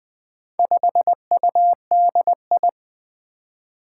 Morse code, heard as a keyed tone, about 20 wpm, 700 Hz. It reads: 5UDI
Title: Four Challenging Letter-Number Combinations 20wpm